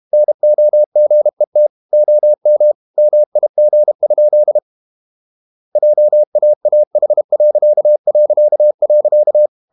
Morse code, heard as a keyed tone, about 32 wpm, 600 Hz. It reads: NOGET OM MIG? JAAH...
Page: Ferie 32 wpm